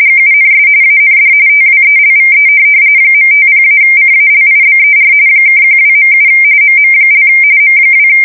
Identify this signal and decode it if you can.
RTTY, about 45.45 baud. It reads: GUNS AND TWELVE .50-CALIBER (12.7 MM) M2
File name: fsk170 / USS-Massachusetts